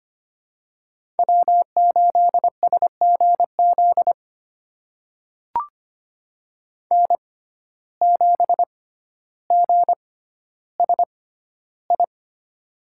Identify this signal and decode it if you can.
Morse code, heard as a keyed tone, about 25 wpm, spaced as for 13 wpm, 700 Hz. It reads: W8SGZ N7GSI
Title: USA Call Signs - 5 Characters - 6x Character Spacing - 25wpm